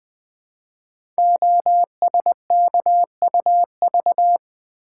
Morse code, heard as a keyed tone, about 20 wpm, 700 Hz. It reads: OSKUV
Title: Code Groups - 5 Letters - 20wpm